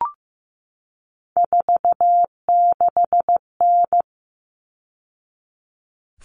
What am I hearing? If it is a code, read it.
Morse code, heard as a keyed tone, about 15 wpm, 700 Hz. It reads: 46N
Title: All Cut Numbers - 3-Digit 15wpm